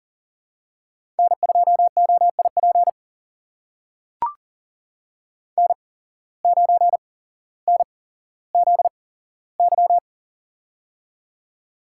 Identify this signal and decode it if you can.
Morse code, heard as a keyed tone, about 38 wpm, spaced as for 17 wpm, 700 Hz. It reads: N2OIP N9NZY
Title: USA Call Signs - 5 Characters - 8x Character Spacing - 40wpm